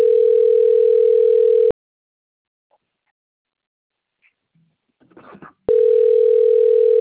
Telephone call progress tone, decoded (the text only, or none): ringback tone